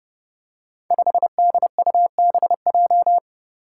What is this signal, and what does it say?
Morse code, heard as a keyed tone, about 30 wpm, 700 Hz. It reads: 5DUBJ
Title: Five Challenging Letter-Number Combinations 30wpm